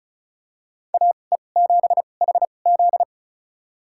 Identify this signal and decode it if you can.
Morse code, heard as a keyed tone, about 35 wpm, spaced as for 25 wpm, 690 Hz. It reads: AE7HZ
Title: USA Call Signs - 5 Characters - 2x Character Spacing - 35wpm